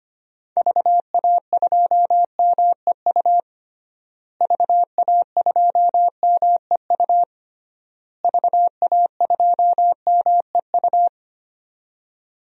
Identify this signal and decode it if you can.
Morse code, heard as a keyed tone, about 25 wpm, 700 Hz. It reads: VA2MEU VA2MEU VA2MEU
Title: DXCC-100 Call Signs - 6 Characters - Repeated 3x - 25wpm